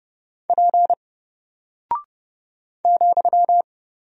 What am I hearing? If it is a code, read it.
Morse code, heard as a keyed tone, about 30 wpm, 700 Hz. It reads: P ,
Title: Single Character (Letter:Number:Punctuation:Prosign) Alt 30wpm